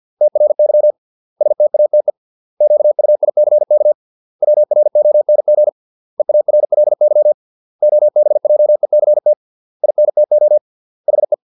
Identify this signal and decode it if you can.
Morse code, heard as a keyed tone, about 50 wpm, 600 Hz. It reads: TRY STATE QUICK WRONG EARLY OBJECT INTO HE
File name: 632 words / 632 Words 50WPM